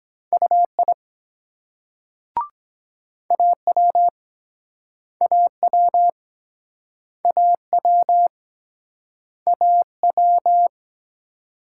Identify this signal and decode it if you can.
Morse code, heard as a keyed tone, about 24 wpm, 700 Hz. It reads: UI AW AW AW AW